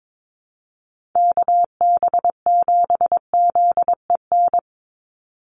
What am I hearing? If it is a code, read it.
Morse code, heard as a keyed tone, about 22 wpm, 700 Hz. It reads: KB7ZEN
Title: USA Call Signs 22wpm